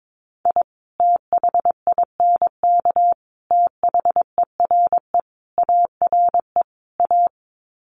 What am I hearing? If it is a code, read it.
Morse code, heard as a keyed tone, about 22 wpm, 700 Hz. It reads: I THINK THERE ARE A
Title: Sets of 5 Words Encore2 22wpm